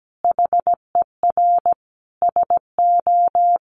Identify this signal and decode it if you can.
Morse code, heard as a keyed tone, about 17 wpm, 700 Hz. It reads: HER SO
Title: Sentences from Top 1500 Words 17wpm